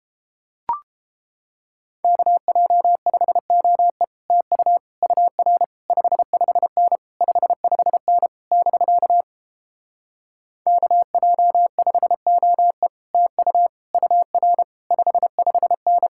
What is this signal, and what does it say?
Morse code, heard as a keyed tone, about 29 wpm, 700 Hz. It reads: KJ5OE TU UR 55N 55N <BK> KJ5OE TU UR 55N